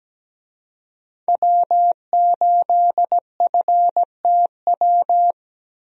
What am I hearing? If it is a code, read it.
Morse code, heard as a keyed tone, about 17 wpm, 700 Hz. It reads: W8FTW
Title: USA Call Signs - 5 Characters - Repeated 2x - 17wpm